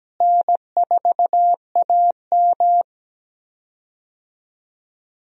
Morse code, heard as a keyed tone, about 17 wpm, 700 Hz. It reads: N4AM
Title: USA Call Signs - 4 Characters - Repeated 3x - 17wpm